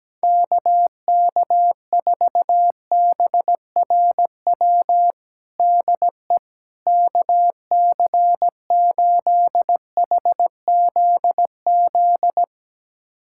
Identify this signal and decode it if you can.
Morse code, heard as a keyed tone, about 17 wpm, 700 Hz. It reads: KK4BRW DE KC8HZZ
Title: USA Callsign DE Callsign - 6 Characters - 17wpm